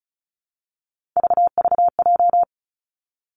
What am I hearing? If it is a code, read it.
Morse code, heard as a keyed tone, about 36 wpm, 700 Hz. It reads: VVJ
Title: Three Challenging Letter-Number Combinations 35wpm